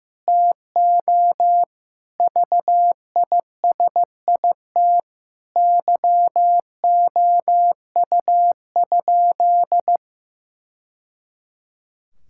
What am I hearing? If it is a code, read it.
Morse code, heard as a keyed tone, about 15 wpm, 700 Hz. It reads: TO VISIT YOU?
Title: Sentences from Top 1000 Words - Review Part 6 - 15wpm